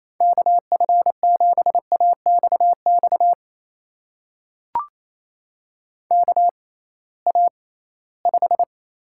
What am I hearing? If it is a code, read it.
Morse code, heard as a keyed tone, about 28 wpm, spaced as for 18 wpm, 700 Hz. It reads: KF7AXX KA5
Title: USA Call Signs - 6 Characters - 6x Character Spacing - 28wpm